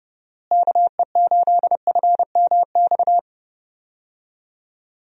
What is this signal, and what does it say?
Morse code, heard as a keyed tone, about 31 wpm, 700 Hz. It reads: KE8FMX
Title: USA Call Signs - 6 Characters - Repeated 3x - 30wpm